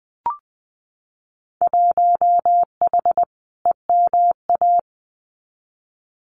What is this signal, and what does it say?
Morse code, heard as a keyed tone, about 20 wpm, 700 Hz. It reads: 1H EMA
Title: Winter Field Day Exchanges 20wpm